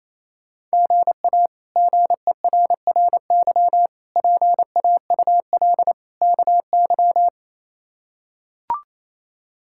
Morse code, heard as a keyed tone, about 28 wpm, 700 Hz. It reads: GA GERRY PAUL KY
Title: K1USN SST QSO Practice - 28wpm